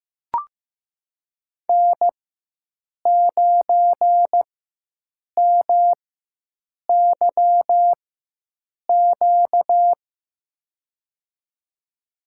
Morse code, heard as a keyed tone, about 15 wpm, spaced as for 7 wpm, 700 Hz. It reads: N9MYQ
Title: USA Call Signs - 4x Character Spacing - 15wpm